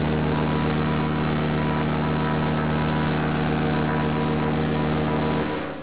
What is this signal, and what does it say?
Morse code, heard as a keyed tone, about 18 wpm, 980 Hz, with noise. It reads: ASTRA PER